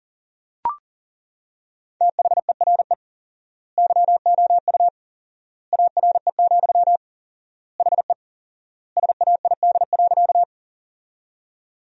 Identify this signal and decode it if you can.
Morse code, heard as a keyed tone, about 40 wpm, 700 Hz. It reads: THERE YOU ARE, HE SAID.
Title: 4x Word-Spacing - Sentences from Top 100 Words - 40wpm